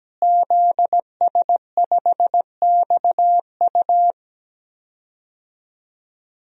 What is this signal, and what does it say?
Morse code, heard as a keyed tone, about 17 wpm, 700 Hz. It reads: ZS5XU
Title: DXCC-100 Call Signs - 5 Characters - Repeated 3x - 17wpm